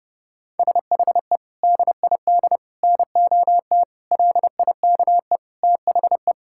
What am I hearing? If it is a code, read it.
Morse code, heard as a keyed tone, about 32 wpm, 700 Hz. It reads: SHE DID NOT LIKE THE
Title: Sentences from Top 1000 Words - Review Part 5 - 30wpm